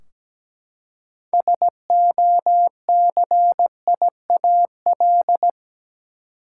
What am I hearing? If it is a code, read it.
Morse code, heard as a keyed tone, about 17 wpm, 700 Hz. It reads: SOCIAL
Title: Practice QSO - Exchange 3 - More Info - 17wpm